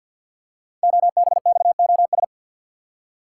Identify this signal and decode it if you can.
Morse code, heard as a keyed tone, about 48 wpm, 700 Hz. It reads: OBXOS
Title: Code Groups - 5 Letters - 50wpm